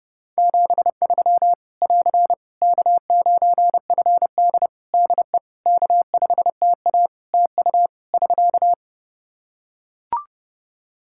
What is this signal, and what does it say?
Morse code, heard as a keyed tone, about 30 wpm, 700 Hz. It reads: 73 <AR> K9FD DE K5TA TU <SK>
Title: K1USN SST QSO Practice 30wpm